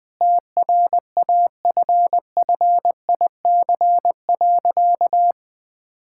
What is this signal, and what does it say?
Morse code, heard as a keyed tone, about 20 wpm, 700 Hz. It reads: TRAFFIC.
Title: Sentences with 8-Word Phrases - 20wpm